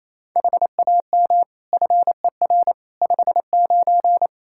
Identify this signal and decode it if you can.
Morse code, heard as a keyed tone, about 28 wpm, 700 Hz. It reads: HAM FER 59
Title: Practice QSO - Exchange 3 - More Info - 28wpm